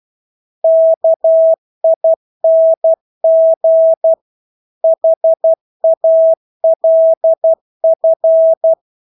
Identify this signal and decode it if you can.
Morse code, heard as a keyed tone, about 12 wpm, 650 Hz. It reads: KING HALF
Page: Engelske ord 12 wpm